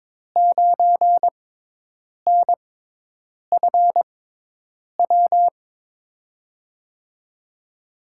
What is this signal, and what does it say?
Morse code, heard as a keyed tone, about 22 wpm, spaced as for 8 wpm, 700 Hz. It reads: 9NFW
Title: USA Call Signs - 5 Characters - 6x Character Spacing - 22wpm